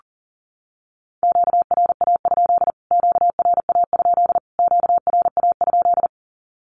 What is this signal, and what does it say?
Morse code, heard as a keyed tone, about 37 wpm, 700 Hz. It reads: QRA? QRA? QRA?
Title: Q-Codes Repeated 3x 40wpm